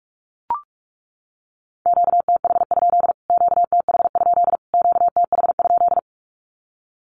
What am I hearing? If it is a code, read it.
Morse code, heard as a keyed tone, about 44 wpm, 700 Hz. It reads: QTH? QTH? QTH?
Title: Q-Codes Repeated 3x 45wpm